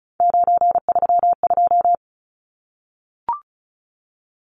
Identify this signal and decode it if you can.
Morse code, heard as a keyed tone, about 34 wpm, 700 Hz. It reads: 932
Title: Numbers - 3-Digit 35wpm